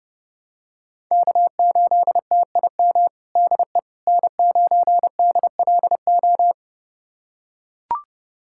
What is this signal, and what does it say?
Morse code, heard as a keyed tone, about 30 wpm, 700 Hz. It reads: K8TIM DE N9DLO
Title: USA Callsign DE Callsign - 5 Characters - 30wpm